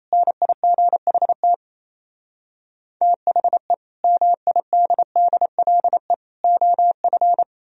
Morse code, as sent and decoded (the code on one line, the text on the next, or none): -. .. --. .... - / - .... / -- .. -.. -.. .-.. / --- ..-.
NIGHT THE MIDDLE OF